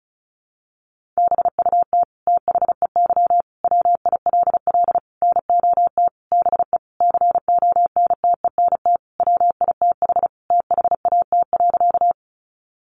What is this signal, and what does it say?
Morse code, heard as a keyed tone, about 34 wpm, 700 Hz. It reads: BUT THEY WILL NOT BE CONTENT WITH THAT.
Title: Sentences from Top 900 Words 35wpm